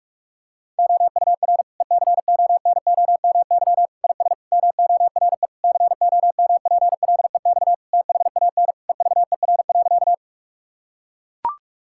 Morse code, heard as a keyed tone, about 41 wpm, 700 Hz. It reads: OUR ECONOMY IS MORE COMPLEX THAN EVER.
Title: Sentences from Top 1000 Words - Review Part 6 - 45wpm